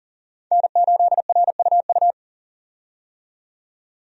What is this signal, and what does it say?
Morse code, heard as a keyed tone, about 37 wpm, 700 Hz. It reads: N8RUU
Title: USA Call Signs - 5 Characters - 40wpm